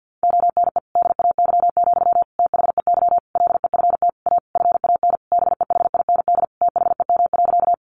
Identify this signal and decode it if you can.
Morse code, heard as a keyed tone, about 50 wpm, 700 Hz. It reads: ONE DAY, THEY LEFT A FAN BEHIND THEM.
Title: Sentences from Top 1300 Words 50wpm